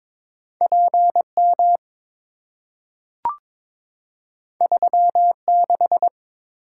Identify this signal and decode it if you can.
Morse code, heard as a keyed tone, about 23 wpm, 700 Hz. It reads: PM 36
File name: practice / 22wpm / Two Letter-Number Combinations - 22wpm